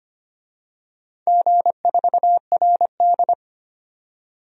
Morse code, heard as a keyed tone, about 25 wpm, 700 Hz. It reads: G4RD
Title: DXCC-100 Call Signs - 4 Characters - 25wpm